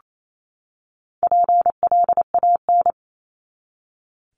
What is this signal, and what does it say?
Morse code, heard as a keyed tone, about 28 wpm, 700 Hz. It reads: PLAN
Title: ING Suffix - Top 100 Words with Base Words 28wpm